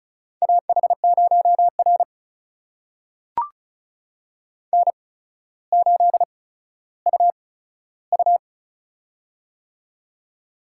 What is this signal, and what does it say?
Morse code, heard as a keyed tone, about 36 wpm, spaced as for 15 wpm, 700 Hz. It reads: AH0R N8UU
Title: USA Call Signs - 4 Characters - 8x Character Spacing - 35wpm